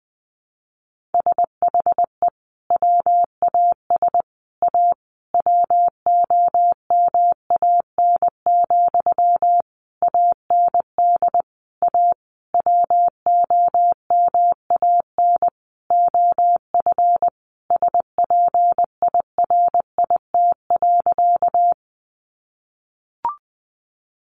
Morse code, heard as a keyed tone, about 20 wpm, 700 Hz. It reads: SHE WAS A WOMAN, AND A WOMAN OF SPIRIT.
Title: Sentences from Top 1400 Words 20wpm